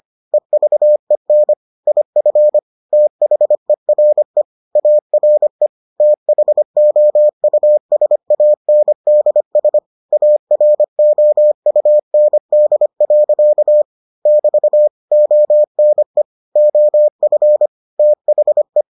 Morse code, heard as a keyed tone, about 25 wpm, 600 Hz. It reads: EVEN IF THERE ARE THOUSANDS AROUND. <BT> ONE OF THE